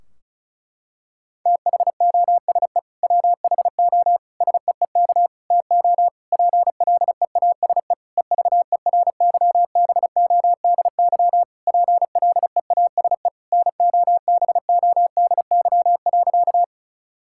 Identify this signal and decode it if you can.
Morse code, heard as a keyed tone, about 34 wpm, 700 Hz. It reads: THOSE WHO SEEK TO PLEASE EVERYBODY PLEASE NOBODY.
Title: Sentences from Top 1600 Words 35wpm